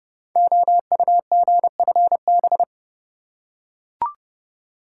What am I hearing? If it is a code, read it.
Morse code, heard as a keyed tone, about 30 wpm, 700 Hz. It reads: OUGFB
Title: Code Groups - 5 Letters - 30wpm